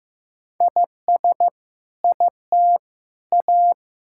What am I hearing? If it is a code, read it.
Morse code, heard as a keyed tone, about 15 wpm, 700 Hz. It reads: IS IT A
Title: Sentences from Top 1000 Words - Review Part 6 - 15wpm